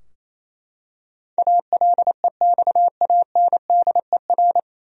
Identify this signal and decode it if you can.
Morse code, heard as a keyed tone, about 28 wpm, 700 Hz. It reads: ALEXANDER
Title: USA Common Names - 28wpm